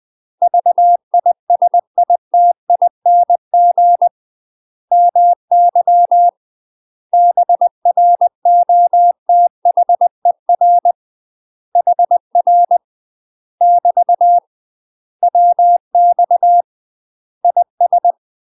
Morse code, heard as a keyed tone, about 20 wpm, 700 Hz. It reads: VISITING MY BROTHER HR <BT> WX IS
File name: UK MASTER QSOs TRAINING / 20WPM / UK MASTER QSOs 0065